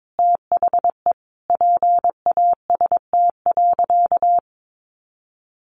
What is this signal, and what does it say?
Morse code, heard as a keyed tone, about 22 wpm, 700 Hz. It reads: THE PAST.
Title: A Sound Of Thunder 22wpm